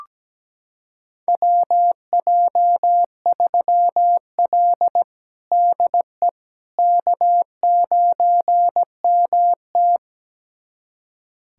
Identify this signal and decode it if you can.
Morse code, heard as a keyed tone, about 17 wpm, 700 Hz. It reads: WJ3L DE K9MT
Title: USA Callsign DE Callsign - 4 Characters - 17wpm